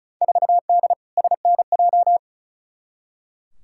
Morse code, heard as a keyed tone, about 34 wpm, 700 Hz. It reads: 4D SNJ
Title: Field Day Exchanges Encore - 35wpm